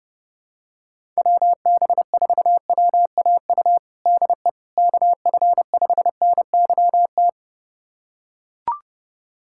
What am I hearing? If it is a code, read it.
Morse code, heard as a keyed tone, about 29 wpm, 700 Hz. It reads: WB4WAU DE KF5NYT
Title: USA Callsign DE Callsign - 6 Characters - 30wpm